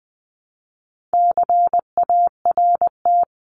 Morse code, heard as a keyed tone, about 20 wpm, 700 Hz. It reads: CART
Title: 260 Lesson 42e - Words 20-20wpm